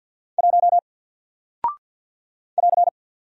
Morse code, heard as a keyed tone, about 48 wpm, 700 Hz. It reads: 1 <AR>
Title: Single Character (Letter:Number:Punctuation:Prosign) Alt 50wpm